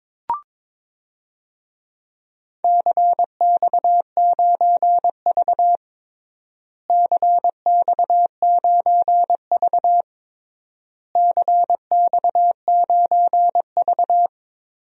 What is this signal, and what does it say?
Morse code, heard as a keyed tone, about 22 wpm, 700 Hz. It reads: CX9V CX9V CX9V
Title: DXCC-100 Call Signs - 4 Characters - Repeated 3x - 22wpm